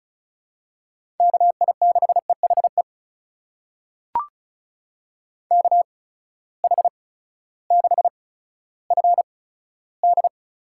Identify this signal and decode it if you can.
Morse code, heard as a keyed tone, about 36 wpm, spaced as for 16 wpm, 700 Hz. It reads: KI6EHE KH6FD
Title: USA Call Signs - 8x Character Spacing - 35wpm